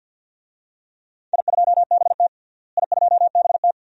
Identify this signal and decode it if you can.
Morse code, heard as a keyed tone, about 52 wpm, 700 Hz. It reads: I2BT I2BT
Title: DXCC-100 Call Signs - 4 Characters - Repeated 2x - 50wpm